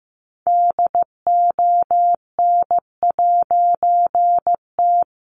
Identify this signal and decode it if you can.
Morse code, heard as a keyed tone, about 15 wpm, 700 Hz. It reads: DON'T
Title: A Sound Of Thunder 15wpm